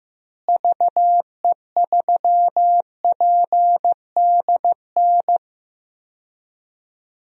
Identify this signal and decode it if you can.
Morse code, heard as a keyed tone, about 15 wpm, 700 Hz. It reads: VE3PDN
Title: DXCC-100 Call Signs - 6 Characters - Repeated 3x - 15wpm